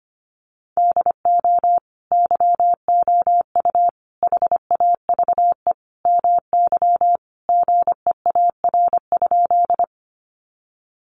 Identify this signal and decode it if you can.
Morse code, heard as a keyed tone, about 25 wpm, 700 Hz. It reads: DO YOU HAVE MY GEAR?
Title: Sentences from Top 1400 Words 25wpm